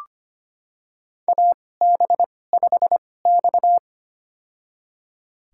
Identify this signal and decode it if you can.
Morse code, heard as a keyed tone, about 25 wpm, spaced as for 18 wpm, 700 Hz. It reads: AB5X
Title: USA Call Signs - 4 Characters - 2x Character Spacing - 25wpm